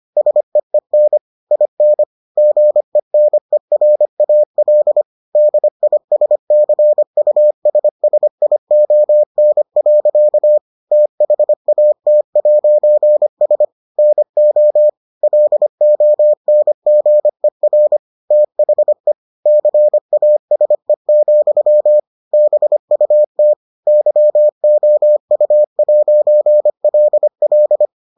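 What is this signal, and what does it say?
Morse code, heard as a keyed tone, about 25 wpm, 600 Hz. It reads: SEEN IN GENERAL DISCUSSION. THAT'S NO LONGER THE CASE, BUT YOU'LL
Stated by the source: Morse edition of the Foundations of Amateur Radio episodes.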